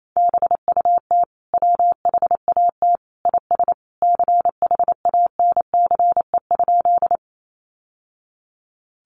Morse code, heard as a keyed tone, about 28 wpm, 700 Hz. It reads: BUT WHAT IS CHANCE?
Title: Sentences from Top 800 Words 28wpm